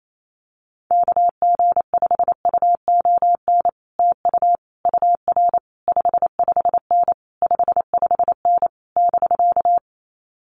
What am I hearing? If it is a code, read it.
Morse code, heard as a keyed tone, about 28 wpm, 700 Hz. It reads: KG5UON TU UR 55N 55N <BK>
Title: Common QSO Phrases Encore 28wpm